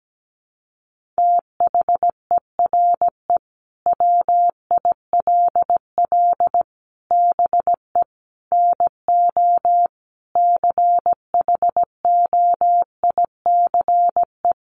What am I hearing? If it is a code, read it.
Morse code, heard as a keyed tone, about 17 wpm, 700 Hz. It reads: THERE WILL BE NO CHOICE
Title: Sentences from Top 1500 Words 17wpm